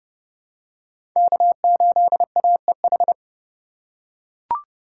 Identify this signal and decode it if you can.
Morse code, heard as a keyed tone, about 32 wpm, 700 Hz. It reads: K8AEH
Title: USA Call Signs - 5 Characters - 4x Character Spacing - 30wpm